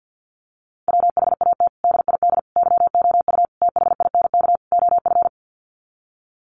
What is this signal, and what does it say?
Morse code, heard as a keyed tone, about 49 wpm, 700 Hz. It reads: WHAT DID YOU THINK OF
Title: Sets of 5 Words Encore2 50wpm